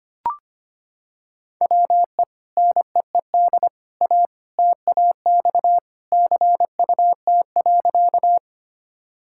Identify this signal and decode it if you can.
Morse code, heard as a keyed tone, about 25 wpm, 700 Hz. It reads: WE NEED A TAX CUT.
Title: Sentences from Top 1000 Words - Review Part 5 - 25wpm